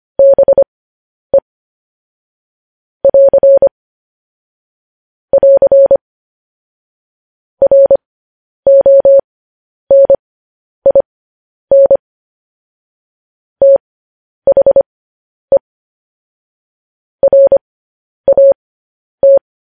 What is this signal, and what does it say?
Morse code, heard as a keyed tone, about 25 wpm, spaced as for 10 wpm, 550 Hz. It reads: BE <AR> <AR> RONIN THE RAT
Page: news headlines in morse code at 10 wpm